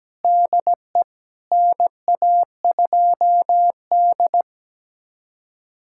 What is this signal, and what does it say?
Morse code, heard as a keyed tone, about 17 wpm, 700 Hz. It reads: DE NA2D
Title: USA Callsign DE Callsign - 4 Characters - 17wpm